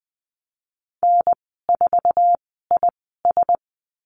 Morse code, heard as a keyed tone, about 19 wpm, spaced as for 14 wpm, 700 Hz. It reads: N4IS
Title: 076 Lesson 13g - Letter L - Call Signs 20wpm